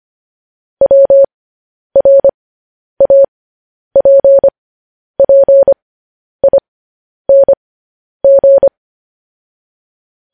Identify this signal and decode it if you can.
Morse code, heard as a keyed tone, about 25 wpm, spaced as for 10 wpm, 550 Hz. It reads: WRAPPING
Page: news-headlines-in-morse-code-at-10-wpm